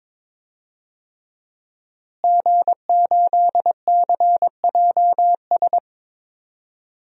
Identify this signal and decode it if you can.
Morse code, heard as a keyed tone, about 22 wpm, 700 Hz. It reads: G8CJS
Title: DXCC-100 Call Signs - 5 Characters - Repeated 3x - 22wpm